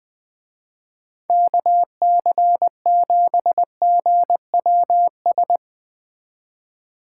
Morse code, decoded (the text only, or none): KC7GWS